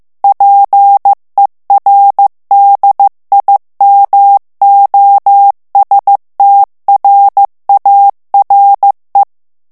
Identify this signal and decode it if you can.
Morse code, heard as a keyed tone, about 15 wpm, 800 Hz. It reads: PER DIMOSTRARE
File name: ITALIEN texte clair 1 vitesse 14 mots minute